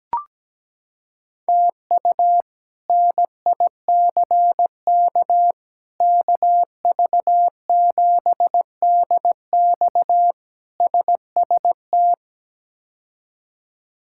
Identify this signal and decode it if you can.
Morse code, heard as a keyed tone, about 17 wpm, 700 Hz. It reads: TU NICK KV7DX SST
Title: K1USN SST QSO Practice - 17wpm